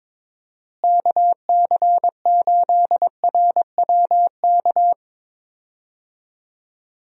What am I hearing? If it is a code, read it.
Morse code, heard as a keyed tone, about 22 wpm, 700 Hz. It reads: KC8RWK
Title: USA Call Signs - 6 Characters - Repeated 2x - 22wpm